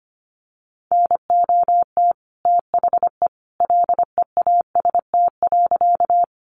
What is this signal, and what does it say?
Morse code, heard as a keyed tone, about 25 wpm, 700 Hz. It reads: NOT THE LEAST.
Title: Sentences from Top 700 Words 25wpm